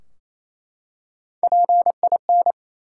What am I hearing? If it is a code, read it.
Morse code, heard as a keyed tone, about 28 wpm, 700 Hz. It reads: PIN
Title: Three Letter Words - 28wpm